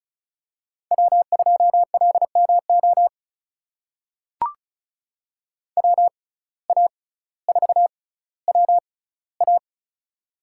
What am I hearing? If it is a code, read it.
Morse code, heard as a keyed tone, about 36 wpm, spaced as for 18 wpm, 700 Hz. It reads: W2LMO WA4WA
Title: USA Call Signs - 5 Characters - 6x Character Spacing - 35wpm